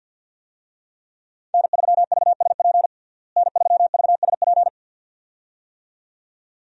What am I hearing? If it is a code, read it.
Morse code, heard as a keyed tone, about 50 wpm, 700 Hz. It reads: N3VSP N3VSP
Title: USA Call Signs - 5 Characters - Repeated 2x - 50wpm